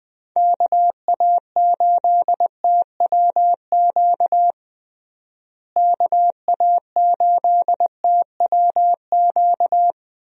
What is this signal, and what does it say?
Morse code, heard as a keyed tone, about 20 wpm, 700 Hz. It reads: KA8TWQ KA8TWQ
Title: USA Call Signs - 6 Characters - Repeated 3x - 20wpm